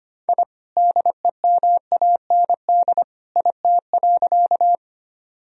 Morse code, heard as a keyed tone, about 25 wpm, 700 Hz. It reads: I DEMAND IT.
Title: Sentences from Top 1000 Words - Review Part 4 - 25wpm